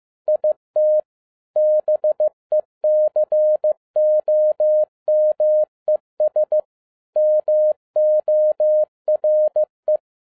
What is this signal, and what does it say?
Morse code, heard as a keyed tone, about 15 wpm, 610 Hz. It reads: IT BECOMES MORE